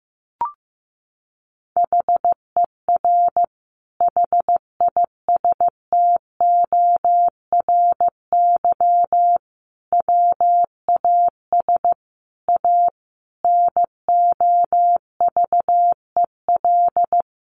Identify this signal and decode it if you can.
Morse code, heard as a keyed tone, about 15 wpm, 700 Hz. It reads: HER HISTORY WAS A NOVEL
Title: Sentences from Top 1300 Words 15wpm